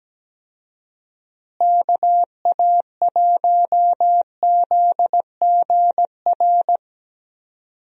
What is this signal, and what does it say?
Morse code, heard as a keyed tone, about 17 wpm, 700 Hz. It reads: KA1ZGR
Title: USA Call Signs - 6 Characters - Repeated 3x - 17wpm